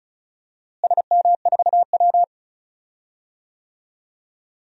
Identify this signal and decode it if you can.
Morse code, heard as a keyed tone, about 36 wpm, 700 Hz. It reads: SM4W
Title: DXCC-100 Call Signs - 4 Characters - Repeated 2x - 35wpm